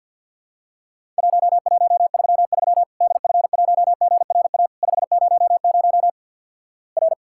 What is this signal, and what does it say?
Morse code, heard as a keyed tone, about 50 wpm, 700 Hz. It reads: 1133 DF1GRA 500 R
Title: Worked All Europe DX Contest - QTC Exchange - 50wpm